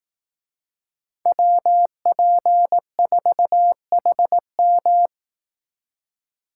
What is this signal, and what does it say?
Morse code, heard as a keyed tone, about 18 wpm, 700 Hz. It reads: WP4HM
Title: Speed Racing - USA 2x2 Call Signs 15wpm